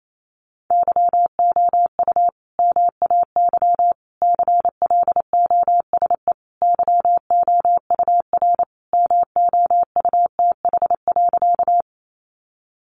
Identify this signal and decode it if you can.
Morse code, heard as a keyed tone, about 28 wpm, 700 Hz. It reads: YOU MAY CLOSE YOUR MOUTH.
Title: Sentences from Top 1500 Words 28wpm